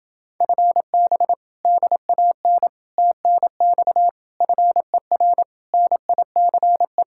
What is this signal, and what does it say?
Morse code, heard as a keyed tone, about 28 wpm, 700 Hz. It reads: FB DAN TNX FER NICE